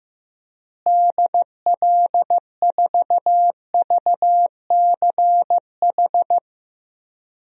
Morse code, heard as a keyed tone, about 15 wpm, 700 Hz. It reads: DL4VCH
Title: DXCC-100 Call Signs - 6 Characters - Repeated 2x - 15wpm